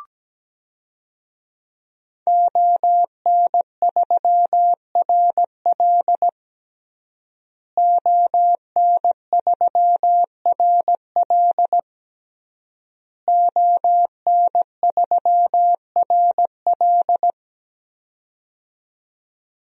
Morse code, heard as a keyed tone, about 17 wpm, 700 Hz. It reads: ON3RL ON3RL ON3RL
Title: DXCC-100 Call Signs - 5 Characters - Repeated 3x - 17wpm